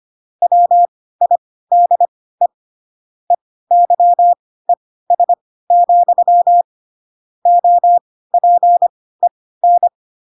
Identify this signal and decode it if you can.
Morse code, heard as a keyed tone, about 25 wpm, spaced as for 16 wpm, 700 Hz. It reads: WIDE EYES, OPEN